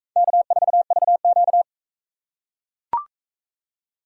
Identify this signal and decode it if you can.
Morse code, heard as a keyed tone, about 40 wpm, 700 Hz. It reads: K4VQ